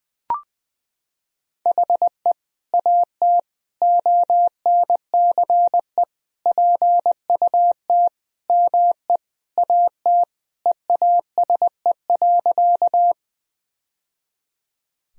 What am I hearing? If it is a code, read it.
Morse code, heard as a keyed tone, about 20 wpm, 700 Hz. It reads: HE AT ONCE PUT ME AT EASE.
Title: Sentences from Top 1000 Words - Review Part 1 - 20wpm